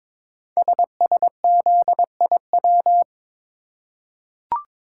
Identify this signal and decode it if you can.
Morse code, heard as a keyed tone, about 22 wpm, 700 Hz. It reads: SSZIW
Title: Code Groups - 5 Letters - 22wpm